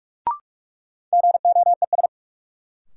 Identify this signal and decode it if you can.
Morse code, heard as a keyed tone, about 46 wpm, 700 Hz. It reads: GOES
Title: Sentences from Top 1500 Words - New Word Drill Alt - 45wpm